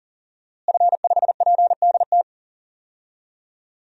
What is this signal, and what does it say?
Morse code, heard as a keyed tone, about 40 wpm, 700 Hz. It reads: F5PDT
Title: DXCC-100 Call Signs - 5 Characters - Repeated 2x - 40wpm